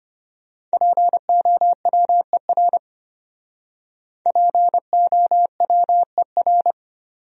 Morse code, heard as a keyed tone, about 32 wpm, 700 Hz. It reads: POWER POWER